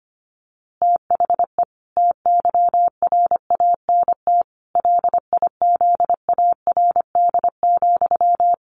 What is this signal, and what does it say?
Morse code, heard as a keyed tone, about 25 wpm, 700 Hz. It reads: THE TYRANT LIZARD,
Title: A Sound Of Thunder 25wpm